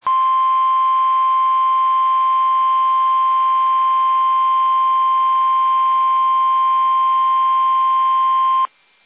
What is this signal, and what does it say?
Emergency Alert System (SAME) tones (the 1050 Hz attention tone). It is heard from 0.0 to 8.7 s.